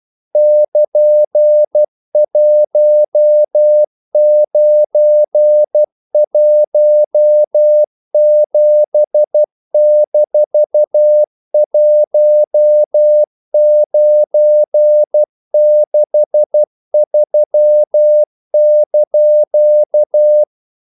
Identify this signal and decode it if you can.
Morse code, heard as a keyed tone, about 12 wpm, 600 Hz. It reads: <KN>1917-1963)
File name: CW / Practice / EnglishQuotations0312